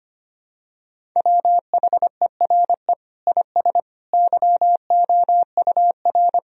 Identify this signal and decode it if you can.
Morse code, heard as a keyed tone, about 25 wpm, 700 Hz. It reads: WHERE IS YOUR
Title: Sentences from Top 1000 Words - Review Part 1 - 25wpm